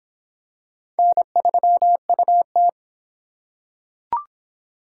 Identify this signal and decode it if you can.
Morse code, heard as a keyed tone, about 25 wpm, 700 Hz. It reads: N3UT